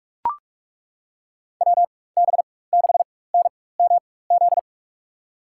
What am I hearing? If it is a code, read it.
Morse code, heard as a keyed tone, about 41 wpm, spaced as for 20 wpm, 700 Hz. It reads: WB6NMZ
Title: USA Call Signs - 6 Characters - 4x Character Spacing - 45wpm